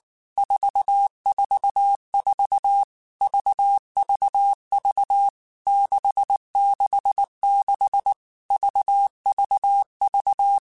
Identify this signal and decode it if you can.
Morse code, heard as a keyed tone, about 19 wpm, 800 Hz. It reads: 444 VVV 666 VVV
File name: Basic Letter Groups By7Groups / VB46 / VB46@18